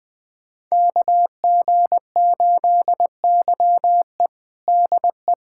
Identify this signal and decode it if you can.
Morse code, heard as a keyed tone, about 21 wpm, 700 Hz. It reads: KG8YE DE
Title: USA Callsign DE Callsign - 5 Characters - 20wpm